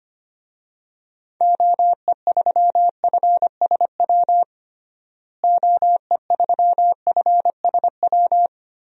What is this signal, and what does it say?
Morse code, heard as a keyed tone, about 25 wpm, 690 Hz. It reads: OE3FSW OE3FSW
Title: DXCC-100 Call Signs - 6 Characters - Repeated 3x - 25wpm